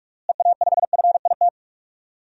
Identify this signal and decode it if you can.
Morse code, heard as a keyed tone, about 46 wpm, 700 Hz. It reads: EA5FIT
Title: DXCC-100 Call Signs - 6 Characters - Repeated 3x - 45wpm